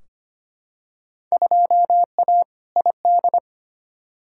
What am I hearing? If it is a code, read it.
Morse code, heard as a keyed tone, about 25 wpm, 700 Hz. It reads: 2A ID
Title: Field Day Exchanges Encore - 25wpm